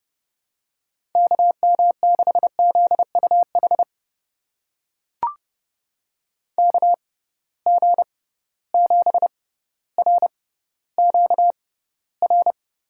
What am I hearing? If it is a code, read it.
Morse code, heard as a keyed tone, about 31 wpm, spaced as for 15 wpm, 700 Hz. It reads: KM6ZUH KG7RQR
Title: USA Call Signs - 6 Characters - 6x Character Spacing - 30wpm